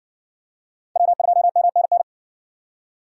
Morse code, heard as a keyed tone, about 58 wpm, 700 Hz. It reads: W2MNN